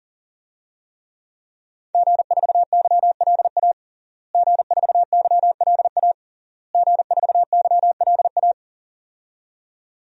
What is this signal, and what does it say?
Morse code, heard as a keyed tone, about 37 wpm, 700 Hz. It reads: G4YLA G4YLA G4YLA
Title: DXCC-100 Call Signs - 5 Characters - Repeated 3x - 40wpm